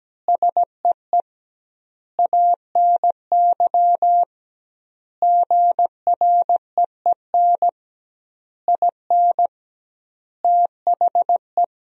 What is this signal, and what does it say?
Morse code, heard as a keyed tone, about 17 wpm, 700 Hz. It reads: SEE ANY GREEN IN THE
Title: 2x Word-Spacing - Sentences from Top 500 Words - 17wpm